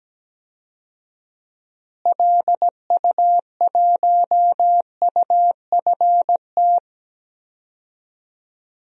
Morse code, heard as a keyed tone, about 17 wpm, 700 Hz. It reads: LU1UFT
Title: DXCC-100 Call Signs - 6 Characters - Repeated 2x - 17wpm